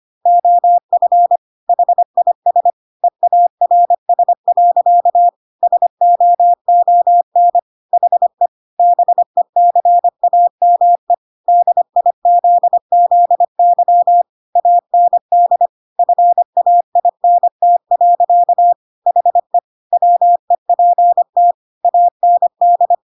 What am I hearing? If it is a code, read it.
Morse code, heard as a keyed tone, about 25 wpm, 700 Hz. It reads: OF HIS EARS. SOON HE BECAME DIZZY AND FAINT. HE WEPT AND
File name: Pinocchio / Pinocchio ch5 25 25 ik1hge